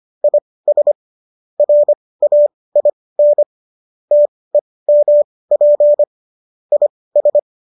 Morse code, heard as a keyed tone, about 25 wpm, spaced as for 18 wpm, 590 Hz. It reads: IS RAIN TEMP IS
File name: qso203 18